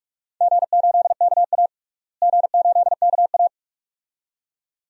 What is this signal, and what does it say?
Morse code, heard as a keyed tone, about 47 wpm, 700 Hz. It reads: G8KA G8KA
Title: DXCC-100 Call Signs - 4 Characters - Repeated 2x - 45wpm